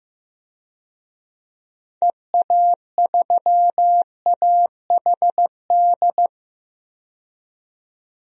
Morse code, heard as a keyed tone, about 15 wpm, 700 Hz. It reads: EA3AHD
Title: DXCC-100 Call Signs - 6 Characters - 15wpm